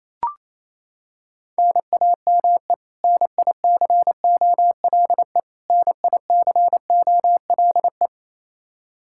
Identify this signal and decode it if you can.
Morse code, heard as a keyed tone, about 28 wpm, 700 Hz. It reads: NAME NICOLE NICOLE
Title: Practice QSO - Exchange 1 - Beginning - 28wpm